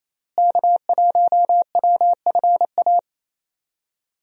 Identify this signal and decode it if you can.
Morse code, heard as a keyed tone, about 28 wpm, 700 Hz. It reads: K1WFA
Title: USA Call Signs - 5 Characters - Repeated 2x - 28wpm